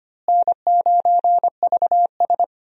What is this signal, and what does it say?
Morse code, heard as a keyed tone, about 25 wpm, 700 Hz. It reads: N9VS
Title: USA Call Signs - 4 Characters - Repeated 2x - 25wpm